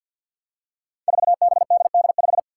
Morse code, heard as a keyed tone, about 50 wpm, 700 Hz. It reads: 4BDD5
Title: Five Challenging Letter-Number Combinations 50wpm